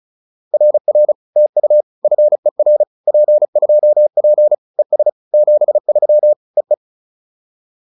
Morse code, heard as a keyed tone, about 34 wpm, 600 Hz. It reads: RR TU FER P2P ES 73 EE
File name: POTA306 35